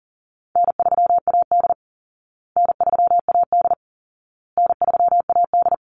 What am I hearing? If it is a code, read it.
Morse code, heard as a keyed tone, about 40 wpm, 700 Hz. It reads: N3AD N3AD N3AD
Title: Easy Call Signs Repeated 3x - 40wpm